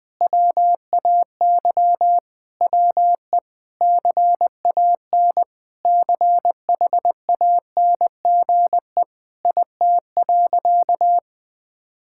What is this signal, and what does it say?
Morse code, heard as a keyed tone, about 21 wpm, 700 Hz. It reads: WAY WE CAN CHANGE IT.
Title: Sentences with 7-Word Phrases - 20wpm